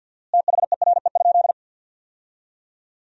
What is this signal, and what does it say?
Morse code, heard as a keyed tone, about 47 wpm, 700 Hz. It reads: THERE?
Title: 4x Word-Spacing - Sentences from Top 300 Words - 50wpm